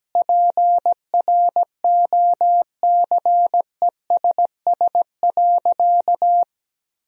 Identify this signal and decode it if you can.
Morse code, heard as a keyed tone, about 17 wpm, 700 Hz. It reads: PROCESS.
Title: Sentences from Top 1000 Words - Review Part 3 - 17wpm